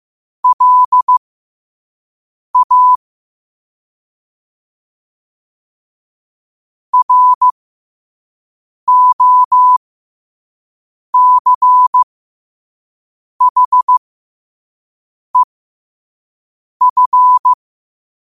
Morse code, heard as a keyed tone, about 15 wpm, spaced as for 5 wpm, 1000 Hz. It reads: LA ROCHEF
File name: 5wpm